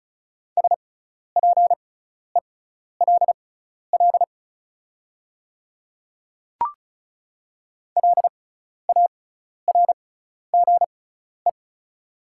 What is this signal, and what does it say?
Morse code, heard as a keyed tone, about 36 wpm, spaced as for 12 wpm, 700 Hz. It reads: SPELL LARGE
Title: 6x Character Spacing - Top 200 Words - 35wpm